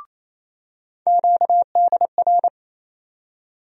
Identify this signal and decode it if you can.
Morse code, heard as a keyed tone, about 28 wpm, 700 Hz. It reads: QDR
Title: USA Call Sign Postfixes - 3 Letters 28wpm